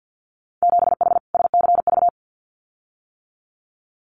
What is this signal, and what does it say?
Morse code, heard as a keyed tone, about 50 wpm, 700 Hz. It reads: 7H SCV
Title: Winter Field Day Exchanges Encore 50wpm